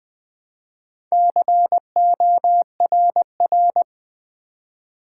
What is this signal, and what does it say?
Morse code, heard as a keyed tone, about 20 wpm, 700 Hz. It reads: CORR
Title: Build A Word - 2 Letters At A Time - 8-Letter Words - 20wpm